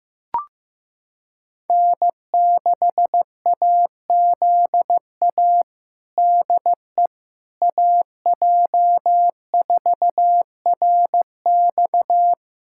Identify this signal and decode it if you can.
Morse code, heard as a keyed tone, about 15 wpm, 700 Hz. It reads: N6AZA DE AJ4RX
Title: USA Callsign DE Callsign - 5 Characters - 15wpm